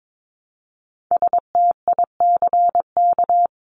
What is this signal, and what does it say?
Morse code, heard as a keyed tone, about 22 wpm, 700 Hz. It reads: STICK
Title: Top 1001-1250 Words 22wpm